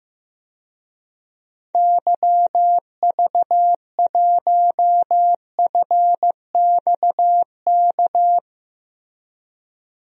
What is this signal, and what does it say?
Morse code, heard as a keyed tone, about 15 wpm, 700 Hz. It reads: YV1FXK
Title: DXCC-100 Call Signs - 6 Characters - Repeated 3x - 15wpm